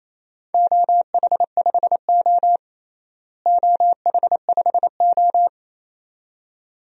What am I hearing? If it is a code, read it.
Morse code, heard as a keyed tone, about 27 wpm, 700 Hz. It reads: OH5O OH5O
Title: DXCC-100 Call Signs - 4 Characters - Repeated 2x - 28wpm